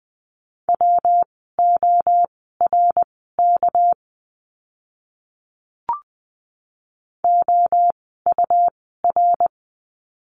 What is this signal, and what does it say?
Morse code, heard as a keyed tone, about 20 wpm, spaced as for 15 wpm, 700 Hz. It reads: WORK OUR
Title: 237 Lesson 38f - Letter q - Words 20wpm